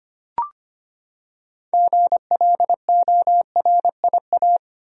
Morse code, heard as a keyed tone, about 25 wpm, 700 Hz. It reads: GLORIA
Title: USA Common Names - 25wpm